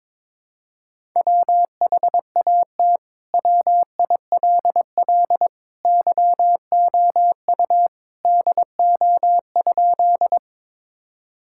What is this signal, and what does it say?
Morse code, heard as a keyed tone, about 22 wpm, 700 Hz. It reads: WHAT WILL YOU DO?
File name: practice / 22wpm / Sentences with 2-Word Phrases - 22wpm